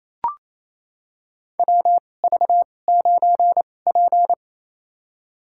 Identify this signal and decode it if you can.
Morse code, heard as a keyed tone, about 28 wpm, spaced as for 20 wpm, 700 Hz. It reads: WV9P
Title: USA Call Signs - 4 Characters - 2x Character Spacing - 28wpm